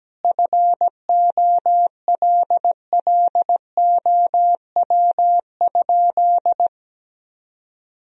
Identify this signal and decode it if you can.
Morse code, heard as a keyed tone, about 17 wpm, 700 Hz. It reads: FOLLOW?
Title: Sentences from Top 1000 Words - Review Part 2 - 17wpm